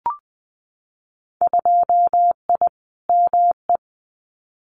Morse code, heard as a keyed tone, about 20 wpm, 700 Hz. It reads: 2I ME
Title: Winter Field Day Exchanges Encore 20wpm